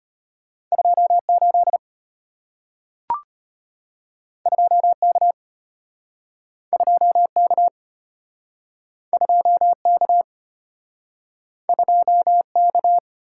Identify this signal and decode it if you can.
Morse code, heard as a keyed tone, about 34 wpm, 690 Hz. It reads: 28 2K 2K 2K 2K